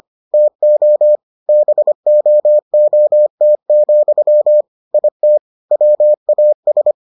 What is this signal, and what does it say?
Morse code, heard as a keyed tone, about 25 wpm, 600 Hz. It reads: TO BOOT, IT WAS